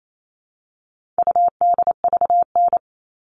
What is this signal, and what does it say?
Morse code, heard as a keyed tone, about 28 wpm, 700 Hz. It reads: UDVN
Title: All Cut Numbers - 4-Digit 28wpm